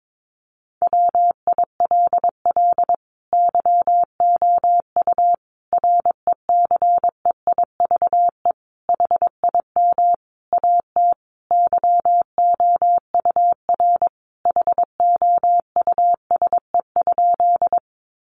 Morse code, heard as a keyed tone, about 22 wpm, 700 Hz. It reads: WILL YOU RECEIVE HIM AT YOUR HOUSE?
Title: Sentences from Top 1000 Words 22wpm